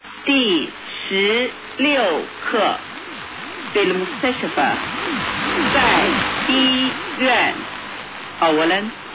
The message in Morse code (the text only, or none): none